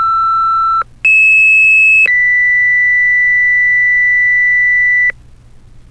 Two-tone page, heard as a one-second tone, then a three-second tone, about 2600 then 2000 Hz, with noise. It is heard from 1.0 to 5.1 s.
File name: fire-pager